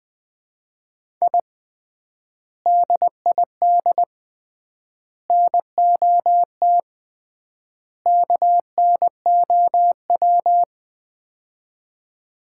Morse code, heard as a keyed tone, about 20 wpm, 700 Hz. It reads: I DID NOT KNOW
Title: 3x Word-Spacing - Sets of 4 Words - 20wpm